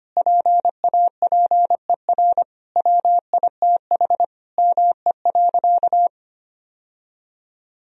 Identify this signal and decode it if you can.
Morse code, heard as a keyed tone, about 25 wpm, 700 Hz. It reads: PAPER WITH ME.
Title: Sentences from Top 1000 Words - Review Part 2 - 25wpm